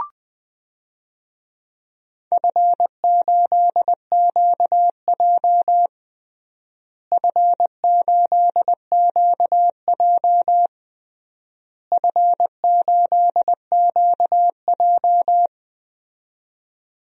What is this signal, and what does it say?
Morse code, heard as a keyed tone, about 20 wpm, 700 Hz. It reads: F8QJ F8QJ F8QJ
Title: DXCC-100 Call Signs - 4 Characters - Repeated 3x - 20wpm